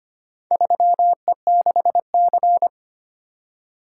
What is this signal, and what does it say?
Morse code, heard as a keyed tone, about 25 wpm, 700 Hz. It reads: 3E6C
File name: practice / 25wpm / Four Letter-Number Combinations - 25wpm